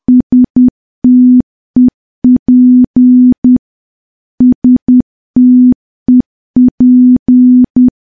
Morse code, heard as a keyed tone, about 10 wpm, 250 Hz. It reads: STEP STEP
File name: morse / morse-training-10WPM